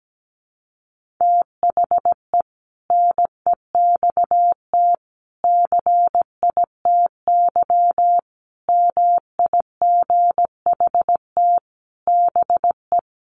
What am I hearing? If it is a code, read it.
Morse code, heard as a keyed tone, about 17 wpm, 700 Hz. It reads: THE NEXT CITY MIGHT BE
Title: Sentences from Top 1400 Words 17wpm